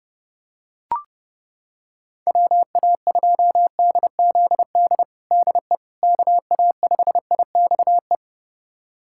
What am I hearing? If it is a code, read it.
Morse code, heard as a keyed tone, about 32 wpm, 700 Hz. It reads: WA2DZD DE KA5IXE
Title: USA Callsign DE Callsign - 6 Characters - 30wpm